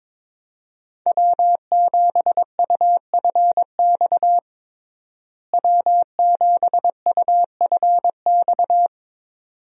Morse code, heard as a keyed tone, about 21 wpm, 700 Hz. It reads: W7UFX W7UFX
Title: USA Call Signs - 5 Characters - Repeated 2x - 22wpm